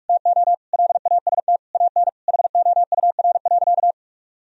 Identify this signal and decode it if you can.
Morse code, heard as a keyed tone, about 44 wpm, 700 Hz. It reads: TO LAST AN HOUR.
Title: Sentences from Top 1000 Words - Review Part 1 - 45wpm